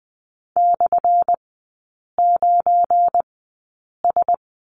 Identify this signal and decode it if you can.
Morse code, heard as a keyed tone, about 20 wpm, spaced as for 8 wpm, 700 Hz. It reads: /9S
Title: 212 Lesson 34e - Number 8 - Three Characters 20wpm